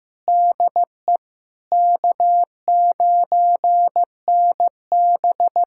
Morse code, heard as a keyed tone, about 15 wpm, 700 Hz. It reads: DE K9NB
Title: USA Callsign DE Callsign - 4 Characters - 15wpm